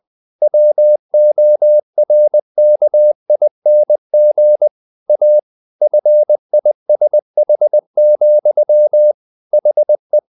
Morse code, heard as a keyed tone, about 20 wpm, 600 Hz. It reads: WORKING A FISH, HE
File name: the old man and the sea / the old man and the sea part 1 20wpm